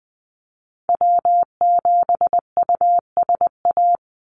Morse code, heard as a keyed tone, about 20 wpm, 700 Hz. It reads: W7USA
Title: NA SOTA Activator Callsigns 20wpm